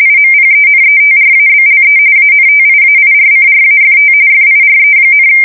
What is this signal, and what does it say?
RTTY, about 45.45 baud. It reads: IMPENETRABLE ANTIAIRCRAFT FIRE,
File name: fsk170 / battleshipnj